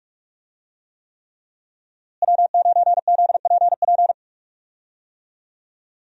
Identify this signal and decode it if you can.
Morse code, heard as a keyed tone, about 43 wpm, 700 Hz. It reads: W9ZPP
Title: USA Call Signs - 5 Characters - Repeated 3x - 45wpm